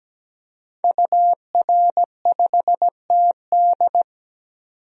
Morse code, heard as a keyed tone, about 17 wpm, 700 Hz. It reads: UR5TD
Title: DXCC-100 Call Signs - 5 Characters - 17wpm